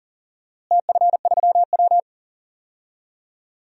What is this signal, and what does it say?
Morse code, heard as a keyed tone, about 40 wpm, 700 Hz. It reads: TF3W
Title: DXCC-100 Call Signs - 4 Characters - Repeated 3x - 40wpm